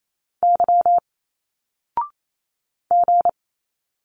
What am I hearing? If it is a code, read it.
Morse code, heard as a keyed tone, about 28 wpm, 700 Hz. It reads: Y G
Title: Single Letters 28wpm